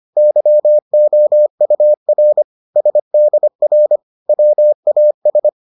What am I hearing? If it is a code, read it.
Morse code, heard as a keyed tone, about 25 wpm, 600 Hz. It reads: YOUR SDR WAS